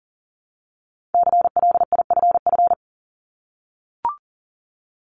Morse code, heard as a keyed tone, about 38 wpm, 700 Hz. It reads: CLIFF
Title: Multi-letter Phonemes 40wpm